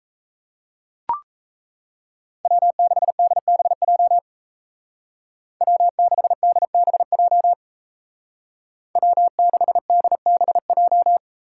Morse code, heard as a keyed tone, about 34 wpm, 700 Hz. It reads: W6DBJ W6DBJ W6DBJ
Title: Speed Racing - USA 1x3 Call Signs 28wpm